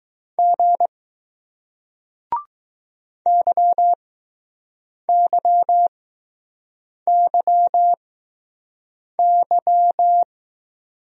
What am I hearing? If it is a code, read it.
Morse code, heard as a keyed tone, about 20 wpm, spaced as for 6 wpm, 700 Hz. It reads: G YYYY